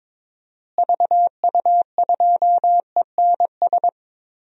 Morse code, heard as a keyed tone, about 22 wpm, 700 Hz. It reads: VU2ENS
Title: DXCC-100 Call Signs - 6 Characters - Repeated 3x - 22wpm